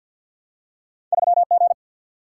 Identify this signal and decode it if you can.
Morse code, heard as a keyed tone, about 53 wpm, 700 Hz. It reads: 3G
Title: Single-Letter Single-Number 50wpm